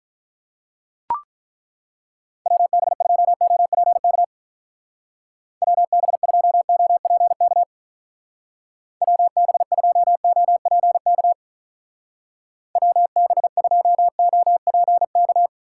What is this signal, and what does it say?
Morse code, heard as a keyed tone, about 43 wpm, 700 Hz. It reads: WB2OPK WB2OPK WB2OPK WB2OPK
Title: Speed Racing - USA Call Signs 35wpm